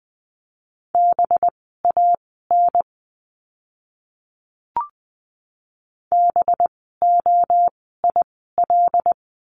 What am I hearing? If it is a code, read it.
Morse code, heard as a keyed tone, about 20 wpm, spaced as for 14 wpm, 700 Hz. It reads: BAN BOIL
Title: 185 Lesson 30f - Letter b - Words 20wpm